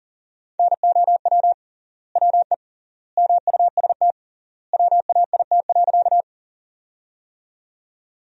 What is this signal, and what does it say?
Morse code, heard as a keyed tone, about 40 wpm, 700 Hz. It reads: NOW WE MUST WAIT.
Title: 3x Word-Spacing - Sentences from Top 500 Words - 40wpm